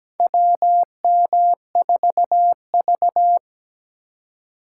Morse code, heard as a keyed tone, about 17 wpm, 700 Hz. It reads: WM4V
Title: USA Call Signs - 4 Characters - 4x Character Spacing - 17wpm